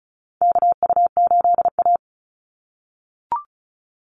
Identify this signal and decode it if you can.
Morse code, heard as a keyed tone, about 34 wpm, 700 Hz. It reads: KU8A
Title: USA Call Signs 35wpm